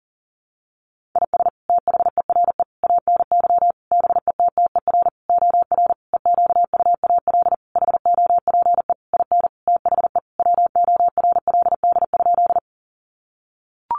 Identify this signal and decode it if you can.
Morse code, heard as a keyed tone, about 40 wpm, 700 Hz. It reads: IS THERE ANY BETTER OR EQUAL HOPE IN THE WORLD?
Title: Sentences from Top 1400 Words 40wpm